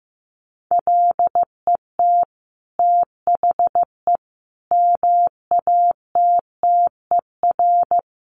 Morse code, heard as a keyed tone, about 15 wpm, 700 Hz. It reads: LET THE MATTER
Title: Sentences from Top 1400 Words 15wpm